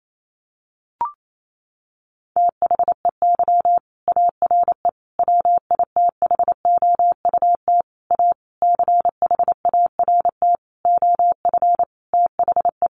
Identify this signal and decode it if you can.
Morse code, heard as a keyed tone, about 28 wpm, 700 Hz. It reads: THEY ARE WITHOUT A CHART OF THE
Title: Sentences from Top 1300 Words 28wpm